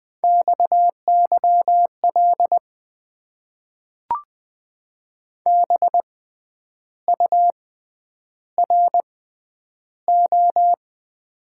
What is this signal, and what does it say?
Morse code, heard as a keyed tone, about 20 wpm, spaced as for 9 wpm, 700 Hz. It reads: XYL BURO
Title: Top 90 QSO Elements - 6x Character Spacing - 20wpm